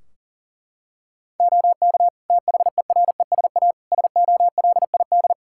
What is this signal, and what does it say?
Morse code, heard as a keyed tone, about 40 wpm, 700 Hz. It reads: OK THERESA SOLID
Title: Practice QSO - Exchange 3 - More Info - 40wpm